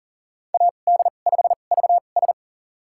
Morse code, heard as a keyed tone, about 38 wpm, spaced as for 28 wpm, 700 Hz. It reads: AD5VS
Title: USA Call Signs - 2x Character Spacing - 40wpm